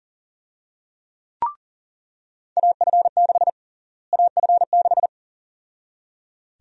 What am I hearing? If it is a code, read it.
Morse code, heard as a keyed tone, about 37 wpm, 700 Hz. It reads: AF6 AF6
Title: USA Call Sign Prefixes - Repeated 2x - 40wpm